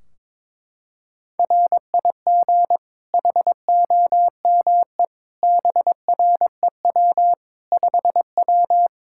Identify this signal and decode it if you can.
Morse code, heard as a keyed tone, about 22 wpm, 700 Hz. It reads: RIG HOME BREW 5W
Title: K1USN SST QSO Practice 22wpm